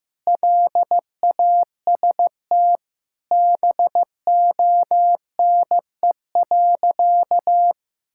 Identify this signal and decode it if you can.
Morse code, heard as a keyed tone, about 16 wpm, 700 Hz. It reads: LAST BONE.
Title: Sentences from Top 1600 Words 15wpm